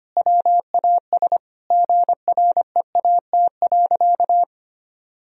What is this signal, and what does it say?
Morse code, heard as a keyed tone, about 25 wpm, 700 Hz. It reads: WAS GREAT.
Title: Sentences from Top 1000 Words - Review Part 4 - 25wpm